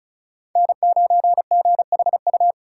Morse code, heard as a keyed tone, about 45 wpm, 700 Hz. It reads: N9GHU